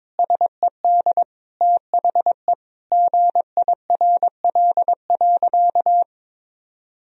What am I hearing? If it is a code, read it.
Morse code, heard as a keyed tone, about 22 wpm, 700 Hz. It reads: SED THE GIRL.
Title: Sentences from Top 400 Words and Abbreviations 22wpm